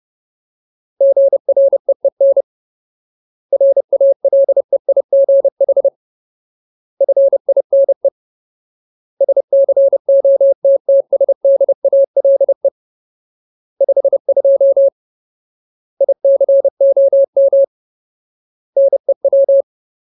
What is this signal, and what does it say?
Morse code, heard as a keyed tone, about 30 wpm, 550 Hz. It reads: GREEN RALEIGH FINE SCOTTSDALE 52 ICOM NEW
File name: Ultimate Word List / 30WPM Ultimate